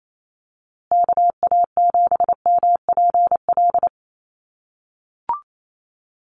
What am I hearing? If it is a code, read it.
Morse code, heard as a keyed tone, about 28 wpm, 700 Hz. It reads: KA7MPL
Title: USA Call Signs Repeated 2x 28wpm